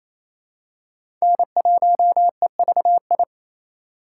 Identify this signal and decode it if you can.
Morse code, heard as a keyed tone, about 28 wpm, 700 Hz. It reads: N1EVI
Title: USA Call Signs - 5 Characters - 2x Character Spacing - 28wpm